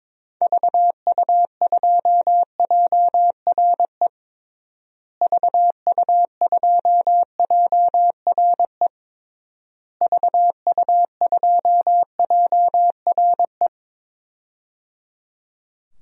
Morse code, heard as a keyed tone, about 22 wpm, 700 Hz. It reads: VU2JRE VU2JRE VU2JRE
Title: DXCC-100 Call Signs - 6 Characters - Repeated 3x - 22wpm